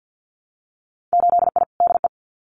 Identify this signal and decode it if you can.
Morse code, heard as a keyed tone, about 50 wpm, 700 Hz. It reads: 8I DE
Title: Winter Field Day Exchanges Encore 50wpm